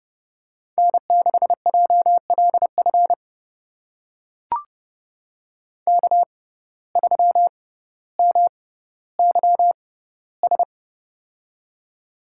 Morse code, heard as a keyed tone, about 29 wpm, spaced as for 15 wpm, 700 Hz. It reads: N6JLF K3MYS
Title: USA Call Signs - 5 Characters - 6x Character Spacing - 30wpm